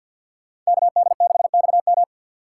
Morse code, heard as a keyed tone, about 64 wpm, 690 Hz. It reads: KD6XM